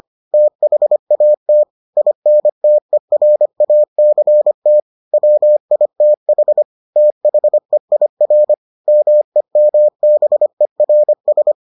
Morse code, heard as a keyed tone, about 25 wpm, 600 Hz. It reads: THAT INTERACT WITH THEIR MEMBERS